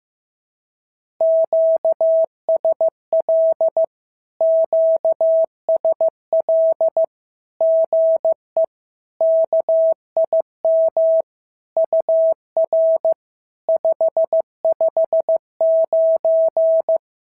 Morse code, heard as a keyed tone, about 15 wpm, 650 Hz. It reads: QSL QSL GE KIM UR 559
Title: SOTA QSO Practice 15wpm